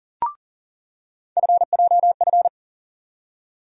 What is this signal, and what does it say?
Morse code, heard as a keyed tone, about 40 wpm, 700 Hz. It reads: FJF
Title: USA Call Sign Postfixes - 3 Letters 40wpm